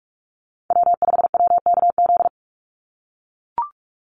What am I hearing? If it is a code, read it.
Morse code, heard as a keyed tone, about 44 wpm, 700 Hz. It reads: W5WKZ
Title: USA Call Signs 45wpm